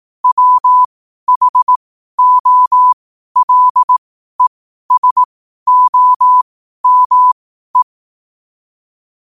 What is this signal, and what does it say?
Morse code, heard as a keyed tone, about 18 wpm, spaced as for 12 wpm, 1000 Hz. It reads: WHOLESOME
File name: qotdfiles / 12wpm